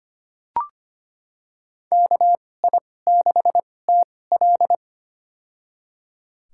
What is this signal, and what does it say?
Morse code, heard as a keyed tone, about 25 wpm, spaced as for 18 wpm, 700 Hz. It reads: KI6TL
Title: USA Call Signs - 2x Character Spacing - 25wpm